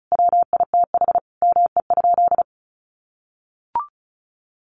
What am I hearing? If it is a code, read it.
Morse code, heard as a keyed tone, about 34 wpm, 700 Hz. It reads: WITH ME?
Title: Sentences from Top 1500 Words 35wpm